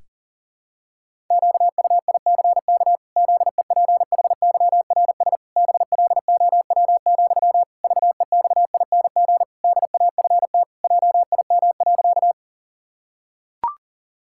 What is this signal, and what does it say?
Morse code, heard as a keyed tone, about 40 wpm, 700 Hz. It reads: QUICK ZEPHYRS BLOW, VEXING DAFT JIM.
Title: Pangram Challenge v2 - 40wpm